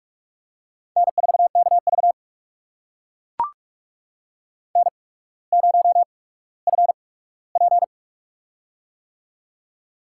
Morse code, heard as a keyed tone, about 44 wpm, spaced as for 19 wpm, 690 Hz. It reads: N4KV N0FP
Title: USA Call Signs - 4 Characters - 8x Character Spacing - 45wpm